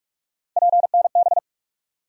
Morse code, heard as a keyed tone, about 46 wpm, 700 Hz. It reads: PNB
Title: USA Call Sign Postfixes - 3 Letters 45wpm